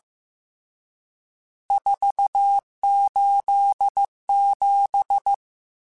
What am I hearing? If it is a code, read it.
Morse code, heard as a keyed tone, about 15 wpm, 800 Hz. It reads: 487
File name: RSTs / RST-mp3s / Various RST @13